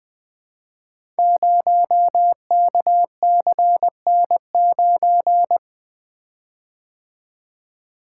Morse code, heard as a keyed tone, about 20 wpm, 700 Hz. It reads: 0KCN9
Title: Code Groups - 5 Letter:Numbers - 20wpm